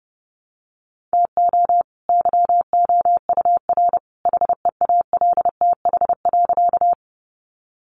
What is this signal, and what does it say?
Morse code, heard as a keyed tone, about 30 wpm, 700 Hz. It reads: TO YOUR HEALTH.
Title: Sentences from Top 600 Words 30wpm